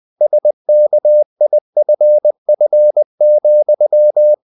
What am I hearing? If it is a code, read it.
Morse code, heard as a keyed tone, about 20 wpm, 600 Hz. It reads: SKIFF,
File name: the old man and the sea part 2 20wpm